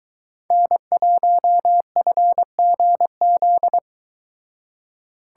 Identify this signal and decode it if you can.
Morse code, heard as a keyed tone, about 22 wpm, 700 Hz. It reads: N1FGZ